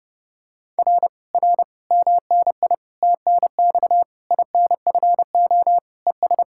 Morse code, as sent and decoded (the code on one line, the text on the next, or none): .-. / .-. / -- -. .. / - -. -..- / .. -. ..-. --- / . ...
R R MNI TNX INFO ES